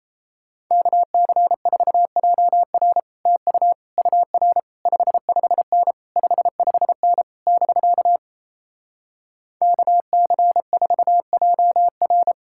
Morse code, heard as a keyed tone, about 37 wpm, 700 Hz. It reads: KC4JR TU UR 55N 55N <BK> KC4JR